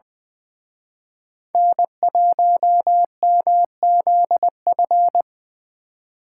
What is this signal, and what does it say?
Morse code, heard as a keyed tone, about 25 wpm, 700 Hz. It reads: N1MZF
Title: Speed Racing - USA 1x3 Call Signs 20wpm